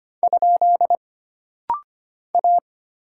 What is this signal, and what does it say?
Morse code, heard as a keyed tone, about 25 wpm, 700 Hz. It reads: ? A
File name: Single Character (Letter:Number:Punctuation:Prosign) - Rapid-Fire - 25wpm